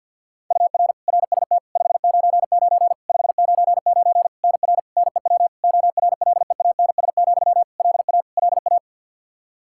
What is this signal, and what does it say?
Morse code, heard as a keyed tone, about 50 wpm, 700 Hz. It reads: UR RST 599 599 NR NEW ORLEANS, LA LA
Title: Practice QSO - Exchange 1 - Beginning - 50wpm